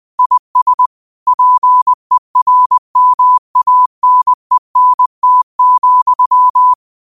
Morse code, heard as a keyed tone, about 20 wpm, 1000 Hz. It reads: IS PERMANENT,